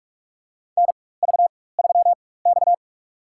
Morse code, heard as a keyed tone, about 46 wpm, spaced as for 21 wpm, 700 Hz. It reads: NV3X
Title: USA Call Signs - 4 Characters - 4x Character Spacing - 45wpm